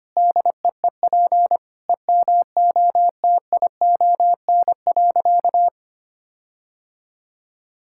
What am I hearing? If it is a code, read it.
Morse code, heard as a keyed tone, about 25 wpm, 700 Hz. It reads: DEEP EMOTION.
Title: Sentences from Top 1600 Words 25wpm